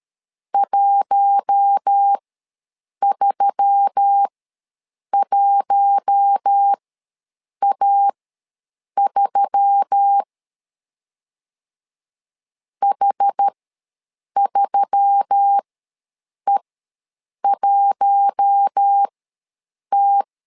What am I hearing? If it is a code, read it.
Morse code, heard as a keyed tone, about 13 wpm, spaced as for 7 wpm, 800 Hz. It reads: 131A3 H3E1T
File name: LEZIONI / 4 Q1S3 ws 12wpm 2038